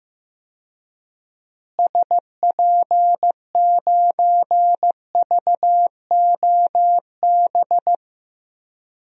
Morse code, heard as a keyed tone, about 15 wpm, 700 Hz. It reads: SP9VOB
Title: DXCC-100 Call Signs - 6 Characters - Repeated 2x - 15wpm